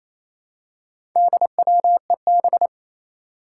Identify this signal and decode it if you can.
Morse code, heard as a keyed tone, about 27 wpm, 700 Hz. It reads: DWEB
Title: Four Challenging Letter-Number Combinations 28wpm